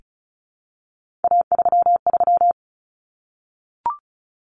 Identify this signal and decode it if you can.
Morse code, heard as a keyed tone, about 34 wpm, 700 Hz. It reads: A33
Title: Common Cut Numbers - 3-Digit 35wpm